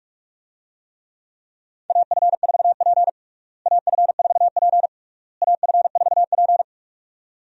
Morse code, heard as a keyed tone, about 47 wpm, 700 Hz. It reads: AF4P AF4P AF4P
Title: USA Call Signs - 4 Characters - Repeated 3x - 45wpm